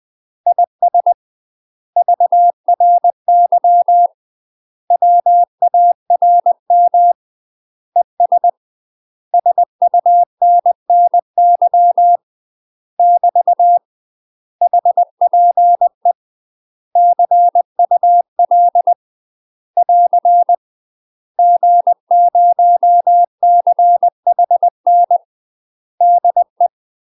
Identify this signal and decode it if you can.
Morse code, heard as a keyed tone, about 20 wpm, 700 Hz. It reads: IS VRY WARM ES SUNNY <BT> HPE CUL <AR> G0CHN DE
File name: UK MASTER QSOs TRAINING / 20WPM / UK MASTER QSOs 0023